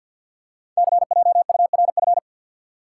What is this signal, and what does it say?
Morse code, heard as a keyed tone, about 50 wpm, 700 Hz. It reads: CJURF
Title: Code Groups - 5 Letters - 50wpm